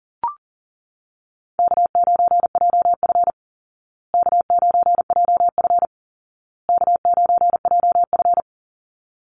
Morse code, heard as a keyed tone, about 37 wpm, 700 Hz. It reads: K9JF K9JF K9JF
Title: Easy Call Signs Repeated 3x - 40wpm